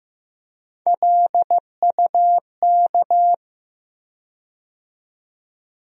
Morse code, heard as a keyed tone, about 15 wpm, 700 Hz. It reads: LUK
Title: USA Call Sign Postfixes - 3 Letters 15wpm